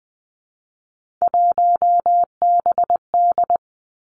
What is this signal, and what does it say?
Morse code, heard as a keyed tone, about 20 wpm, 700 Hz. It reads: 1BD
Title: Three Challenging Letter-Number Combinations 20wpm